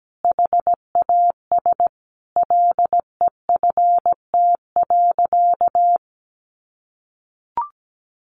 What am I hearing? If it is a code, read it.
Morse code, heard as a keyed tone, about 17 wpm, 700 Hz. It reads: HAS LEFT.
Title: Sentences from Top 1500 Words 17wpm